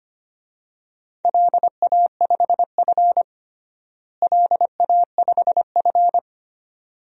Morse code, heard as a keyed tone, about 25 wpm, 700 Hz. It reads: LA5F LA5F
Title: DXCC-100 Call Signs - 4 Characters - Repeated 2x - 25wpm